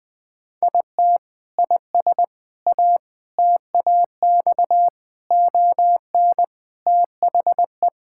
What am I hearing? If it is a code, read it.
Morse code, heard as a keyed tone, about 19 wpm, 700 Hz. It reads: IT IS A TAX ON THE
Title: Sentences from Top 1000 Words - Review Part 5 - 20wpm